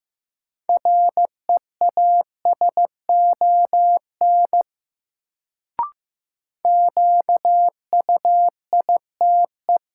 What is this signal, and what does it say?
Morse code, heard as a keyed tone, about 15 wpm, 700 Hz. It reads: REASON QUITE
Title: Sentences from Top 700 Words - New Word Drill - 15wpm